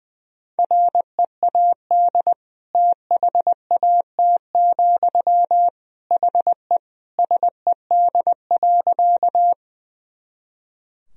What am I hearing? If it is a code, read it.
Morse code, heard as a keyed tone, about 20 wpm, 700 Hz. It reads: READ THAT, HE SED.
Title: Sentences from Top 200 Words and Abbreviations 20wpm